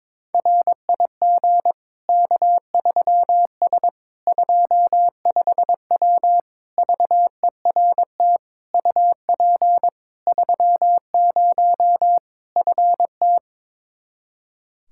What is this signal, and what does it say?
Morse code, heard as a keyed tone, about 22 wpm, 700 Hz. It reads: RIG K3S 25W VERT UP 30 FT
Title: Practice QSO - Exchange 2 - Info - 22wpm